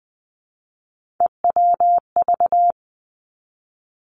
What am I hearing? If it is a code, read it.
Morse code, heard as a keyed tone, about 20 wpm, 700 Hz. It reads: EWV
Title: Three Challenging Letter-Number Combinations 20wpm